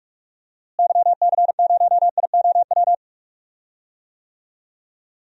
Morse code, heard as a keyed tone, about 41 wpm, 700 Hz. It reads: YC0IOW
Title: DXCC-100 Call Signs - 6 Characters - Repeated 3x - 45wpm